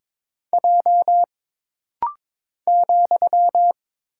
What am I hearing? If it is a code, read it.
Morse code, heard as a keyed tone, about 22 wpm, 700 Hz. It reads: J ,
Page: This rapid-fire timing is geared to push your ICR (Instant Character Recognition) skills.